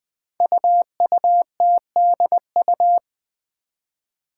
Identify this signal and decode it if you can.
Morse code, heard as a keyed tone, about 20 wpm, 700 Hz. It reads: UUTDU
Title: Code Groups - 5 Letters - 20wpm